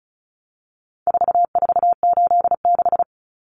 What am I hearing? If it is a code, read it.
Morse code, heard as a keyed tone, about 34 wpm, 700 Hz. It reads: 4486
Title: Numbers - 4-Digit 35wpm